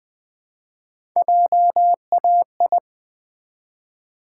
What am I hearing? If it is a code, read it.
Morse code, heard as a keyed tone, about 20 wpm, 700 Hz. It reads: JAI
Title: Build A Word - 1 Letter At A Time - 3 to 5-Letter Words - 20wpm